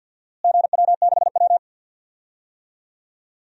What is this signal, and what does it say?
Morse code, heard as a keyed tone, about 48 wpm, 700 Hz. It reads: GW6W
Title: DXCC-100 Call Signs - 4 Characters - Repeated 2x - 50wpm